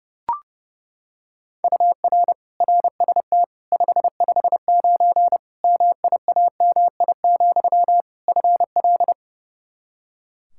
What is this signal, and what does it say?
Morse code, heard as a keyed tone, about 30 wpm, 700 Hz. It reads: UR RST 559 MIAMI, FL
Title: K1USN SST QSO Practice 30wpm